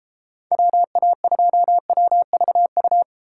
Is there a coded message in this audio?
Yes: Morse code from 0.5 to 3.0 s, heard as a keyed tone, about 33 wpm, 700 Hz. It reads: WA2WVU